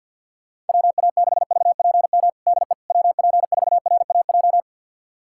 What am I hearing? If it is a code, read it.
Morse code, heard as a keyed tone, about 51 wpm, 700 Hz. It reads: WA6VPM DE WP4RAJ
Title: USA Callsign DE Callsign - 6 Characters - 50wpm